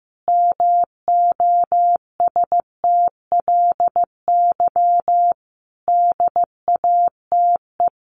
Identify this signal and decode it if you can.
Morse code, heard as a keyed tone, about 15 wpm, 700 Hz. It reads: MOSTLY DATE
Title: Sentences from Top 1300 Words 15wpm